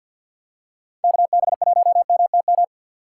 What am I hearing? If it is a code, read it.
Morse code, heard as a keyed tone, about 51 wpm, 700 Hz. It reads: KB1MTM
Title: USA Call Signs - 6 Characters - 4x Character Spacing - 50wpm